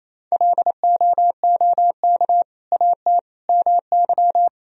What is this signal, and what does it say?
Morse code, heard as a keyed tone, about 27 wpm, 700 Hz. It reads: LOOK AT MY
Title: Sentences from Top 1000 Words - Review Part 3 - 28wpm